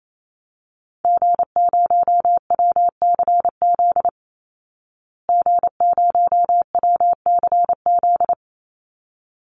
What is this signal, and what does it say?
Morse code, heard as a keyed tone, about 27 wpm, 700 Hz. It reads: G0WCZ G0WCZ
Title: Call Signs Repeated 3x - 28wpm